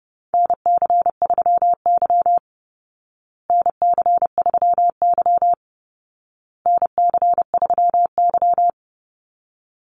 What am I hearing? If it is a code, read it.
Morse code, heard as a keyed tone, about 30 wpm, 700 Hz. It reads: NC3Y NC3Y NC3Y
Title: Call Signs Repeated 2x - 30wpm